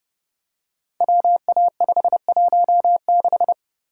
Morse code, heard as a keyed tone, about 30 wpm, 700 Hz. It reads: WA516
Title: Code Groups - 5 Letter:Numbers - 30wpm